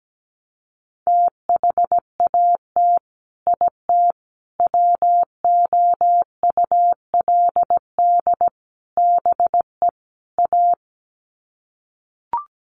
Morse code, heard as a keyed tone, about 17 wpm, 700 Hz. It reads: THAT IT WOULD BE A
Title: Sets of 5 Words Encore2 17wpm